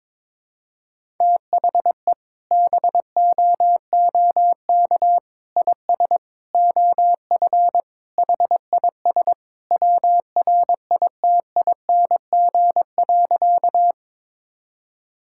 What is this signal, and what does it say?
Morse code, heard as a keyed tone, about 22 wpm, 700 Hz. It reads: THE BOOK IS OF HIS WRITING.
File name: practice / 22wpm / Sentences with 2-Word Phrases - 22wpm